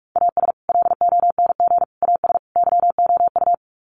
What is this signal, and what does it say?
Morse code, heard as a keyed tone, about 44 wpm, 700 Hz. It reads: AS LONG AS YOU
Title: Sets of 4 Words 45wpm